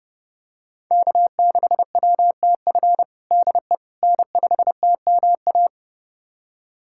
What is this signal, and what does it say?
Morse code, heard as a keyed tone, about 32 wpm, 700 Hz. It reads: K6WTF DE N5TMA
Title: USA Callsign DE Callsign - 5 Characters - 30wpm